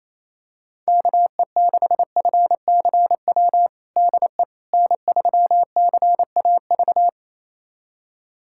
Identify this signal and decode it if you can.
Morse code, heard as a keyed tone, about 28 wpm, 700 Hz. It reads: KE6FCW DE N3CAV
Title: USA Callsign DE Callsign 28wpm